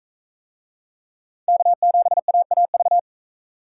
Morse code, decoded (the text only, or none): K7AAV